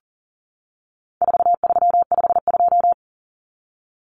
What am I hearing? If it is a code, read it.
Morse code, heard as a keyed tone, about 40 wpm, 700 Hz. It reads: 4352
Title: Common Cut Numbers - 4-Digit 40wpm